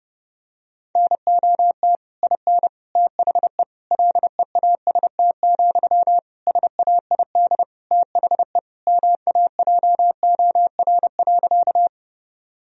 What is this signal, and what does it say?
Morse code, heard as a keyed tone, about 31 wpm, 700 Hz. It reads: NOT IN THE LEAST, SAID THE MAJOR.
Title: Sentences from Top 1000 Words - Review Part 4 - 30wpm